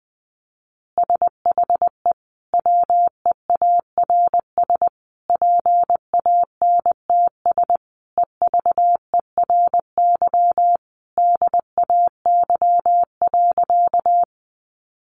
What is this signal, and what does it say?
Morse code, heard as a keyed tone, about 20 wpm, 700 Hz. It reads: SHE WEARS PANTS EVERY DAY.
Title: Additional Easy Phrases 20wpm